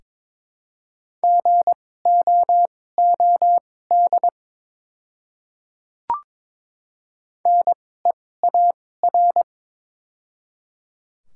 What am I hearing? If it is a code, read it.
Morse code, heard as a keyed tone, about 22 wpm, spaced as for 16 wpm, 700 Hz. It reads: GOOD NEAR
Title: 2x Character Spacing - Top 200 Words - 22wpm